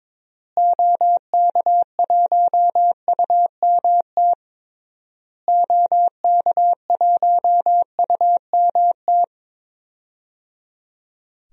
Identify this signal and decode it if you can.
Morse code, heard as a keyed tone, about 22 wpm, 700 Hz. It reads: OK1UMT OK1UMT
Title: DXCC-100 Call Signs - 6 Characters - Repeated 3x - 22wpm